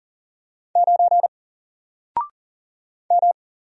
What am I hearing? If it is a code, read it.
Morse code, heard as a keyed tone, about 40 wpm, 700 Hz. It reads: 9 M
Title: Single Letter-Number Alt 40wpm